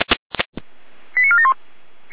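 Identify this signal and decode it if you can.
Five-tone selective call (ZVEI), heard as quick runs of five tones, tones of about 70 ms. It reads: 89471
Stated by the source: ZVEI five-tone call received on an SDR.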